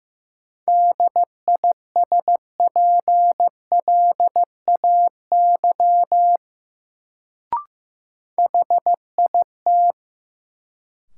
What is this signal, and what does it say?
Morse code, heard as a keyed tone, about 16 wpm, 700 Hz. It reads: DISPLAY HIT
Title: Sentences from Top 1200 Words - New Word Drill Alt - 15wpm